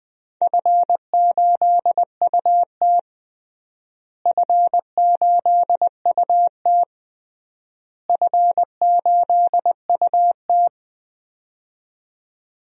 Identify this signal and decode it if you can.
Morse code, heard as a keyed tone, about 20 wpm, 700 Hz. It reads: F8UT F8UT F8UT
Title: DXCC-100 Call Signs - 4 Characters - Repeated 3x - 20wpm